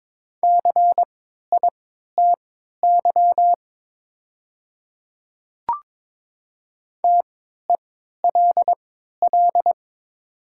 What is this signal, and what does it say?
Morse code, heard as a keyed tone, about 22 wpm, spaced as for 12 wpm, 700 Hz. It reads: CITY TELL
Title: 3x Character Spacing - Top 500 Words - 22wpm